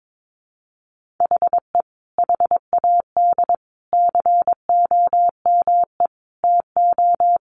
Morse code, heard as a keyed tone, about 22 wpm, 700 Hz. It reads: HE HAD COME TO
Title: Sentences from Top 1500 Words 22wpm